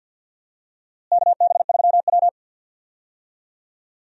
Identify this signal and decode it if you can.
Morse code, heard as a keyed tone, about 52 wpm, 700 Hz. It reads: KB3W
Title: USA Call Signs - 4 Characters - Repeated 2x - 50wpm